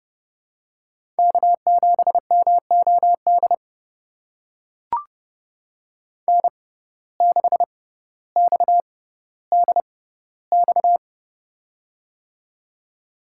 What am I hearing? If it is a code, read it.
Morse code, heard as a keyed tone, about 30 wpm, spaced as for 15 wpm, 700 Hz. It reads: K7MOD N6XDX
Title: USA Call Signs - 5 Characters - 6x Character Spacing - 30wpm